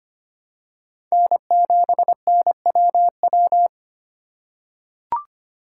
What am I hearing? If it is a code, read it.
Morse code, heard as a keyed tone, about 25 wpm, 700 Hz. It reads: N7NWW
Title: USA Call Signs - 5 Characters - 3x Character Spacing - 25wpm